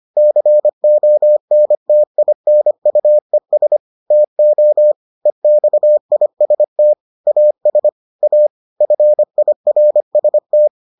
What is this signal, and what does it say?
Morse code, heard as a keyed tone, about 25 wpm, 600 Hz. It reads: CONTINUES TO EXIST AS A FIRST